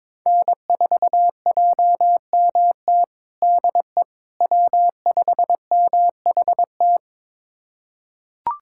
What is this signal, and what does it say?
Morse code, heard as a keyed tone, about 22 wpm, 700 Hz. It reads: N4JMT DE W5MHT
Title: USA Callsign DE Callsign - 5 Characters - 22wpm